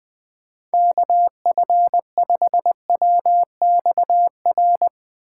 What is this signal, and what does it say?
Morse code, heard as a keyed tone, about 21 wpm, 700 Hz. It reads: KF5WXR
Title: USA Call Signs - 3x Character Spacing - 20wpm